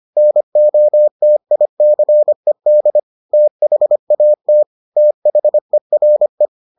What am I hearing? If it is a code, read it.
Morse code, heard as a keyed tone, about 25 wpm, 600 Hz. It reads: NOTICED THAT THERE